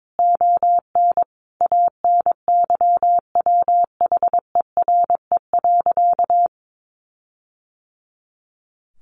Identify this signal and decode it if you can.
Morse code, heard as a keyed tone, about 22 wpm, 700 Hz. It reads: ON ANYWHERE.
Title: Sentences from Top 1400 Words 22wpm